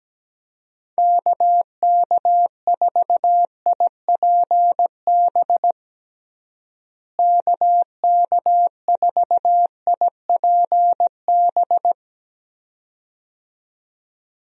Morse code, heard as a keyed tone, about 17 wpm, 700 Hz. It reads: KK4IPB KK4IPB
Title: USA Call Signs - 6 Characters - Repeated 2x - 17wpm